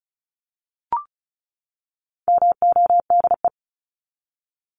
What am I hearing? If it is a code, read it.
Morse code, heard as a keyed tone, about 34 wpm, 700 Hz. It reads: MODE
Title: Top 1751-2000 Words 35wpm